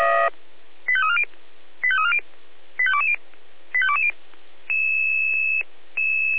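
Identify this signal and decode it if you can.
Five-tone selective call (ZVEI), heard as quick runs of five tones, tones of about 80 ms. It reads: 85340 85340 85220 85220
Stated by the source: ZVEI five-tone call received by SDR.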